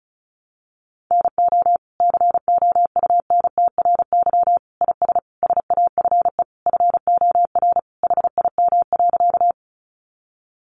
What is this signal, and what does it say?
Morse code, heard as a keyed tone, about 34 wpm, 700 Hz. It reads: NO COUNTRY IS SAFE FOR HIM.
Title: Sentences from Top 900 Words 35wpm